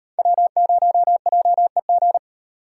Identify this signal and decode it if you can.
Morse code, heard as a keyed tone, about 35 wpm, 700 Hz. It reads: W0JEG